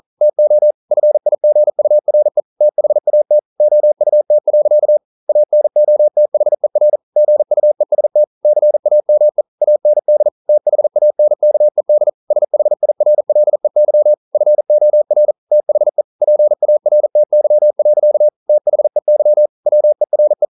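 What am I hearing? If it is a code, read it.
Morse code, heard as a keyed tone, about 40 wpm, 600 Hz. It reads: TO FIGURE THAT OUT. ANOTHER GUEST CAME AND THANKED SHIRLEY FOR THE PARTY. THEY WERE
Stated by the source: Morse code at 40 wpm, practice text, CW med Gnister.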